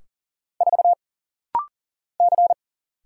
Morse code, heard as a keyed tone, about 40 wpm, 700 Hz. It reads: 4 C
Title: Single Letter-Number Alt - Rapid-Fire 40wpm